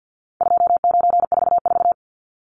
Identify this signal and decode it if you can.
Morse code, heard as a keyed tone, about 50 wpm, 700 Hz. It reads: 2944
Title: Numbers - 4-Digit 50wpm